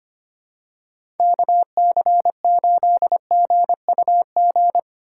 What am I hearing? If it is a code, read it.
Morse code, heard as a keyed tone, about 25 wpm, 700 Hz. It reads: KC8GUG
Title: USA Call Signs - 6 Characters - 4x Character Spacing - 25wpm